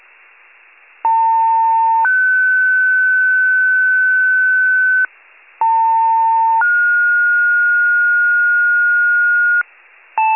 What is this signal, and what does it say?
Two-tone page, heard as a one-second tone, then a three-second tone, about 900 then 1500 Hz, about 900 then 1430 Hz. It is heard from 1.0 to 9.6 s.